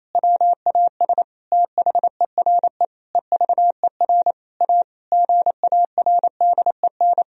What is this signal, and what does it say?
Morse code, heard as a keyed tone, about 28 wpm, 700 Hz. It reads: WAS THERE EVER A GARDEN
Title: Sentences from Top 1000 Words - Review Part 4 - 28wpm